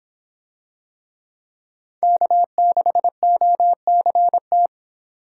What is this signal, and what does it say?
Morse code, heard as a keyed tone, about 22 wpm, 700 Hz. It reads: K6OCT